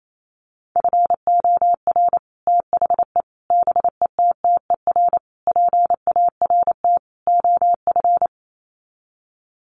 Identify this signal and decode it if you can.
Morse code, heard as a keyed tone, about 28 wpm, 700 Hz. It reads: FOR THE BETTER PART OF
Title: Sets of 5 Words Encore2 28wpm